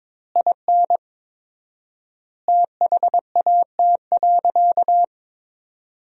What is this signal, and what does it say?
Morse code, heard as a keyed tone, about 22 wpm, 700 Hz. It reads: IN THAT.
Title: 4x Word-Spacing - Sentences from Top 200 Words - 22wpm